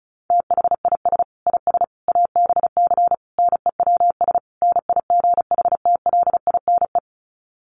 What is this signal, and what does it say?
Morse code, heard as a keyed tone, about 34 wpm, 700 Hz. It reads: THIS IS ABC NEWS NIGHTLINE
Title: Sets of 5 Words 35wpm